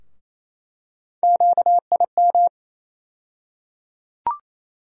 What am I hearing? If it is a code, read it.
Morse code, heard as a keyed tone, about 28 wpm, 700 Hz. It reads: QIM
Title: USA Call Sign Postfixes - 3 Letters 28wpm